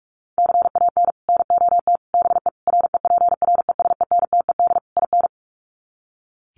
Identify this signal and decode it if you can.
Morse code, heard as a keyed tone, about 44 wpm, 700 Hz. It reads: CAN NOT BE REPRESENTED IN
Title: Sets of 5 Words 45wpm